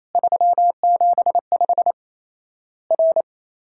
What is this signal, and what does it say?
Morse code, heard as a keyed tone, about 28 wpm, 700 Hz. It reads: 375 R
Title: Worked All Europe DX Contest - QTC Exchange - 28wpm